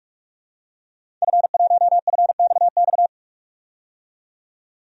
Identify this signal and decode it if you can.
Morse code, heard as a keyed tone, about 47 wpm, 700 Hz. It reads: F1FXX
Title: DXCC-100 Call Signs - 5 Characters - 45wpm